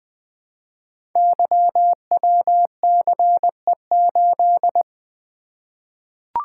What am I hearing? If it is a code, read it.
Morse code, heard as a keyed tone, about 20 wpm, 700 Hz. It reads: YWCE8
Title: Code Groups - 5 Letter:Numbers - 20wpm